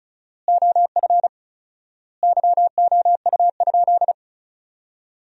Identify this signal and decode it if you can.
Morse code, heard as a keyed tone, about 36 wpm, 700 Hz. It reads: OF YOU?
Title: 4x Word-Spacing - Sentences from Top 300 Words - 35wpm